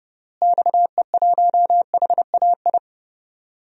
Morse code, heard as a keyed tone, about 30 wpm, 700 Hz. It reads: XE1HAI
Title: DXCC-100 Call Signs - 6 Characters - Repeated 3x - 30wpm